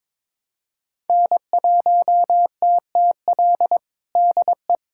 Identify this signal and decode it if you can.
Morse code, heard as a keyed tone, about 22 wpm, 700 Hz. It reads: N1TTL DE
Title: USA Callsign DE Callsign - 5 Characters - 22wpm